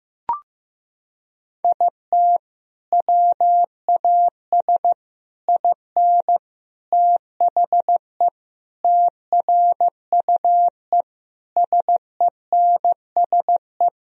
Morse code, heard as a keyed tone, about 16 wpm, 700 Hz. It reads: IT WAS IN THE TRUE SENSE
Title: Sentences from Top 1000 Words - Review Part 4 - 15wpm